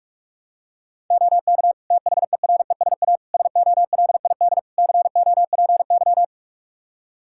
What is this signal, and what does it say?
Morse code, heard as a keyed tone, about 46 wpm, 700 Hz. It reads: OK THERESA SOLID COPY
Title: Practice QSO - Exchange 3 - More Info - 45wpm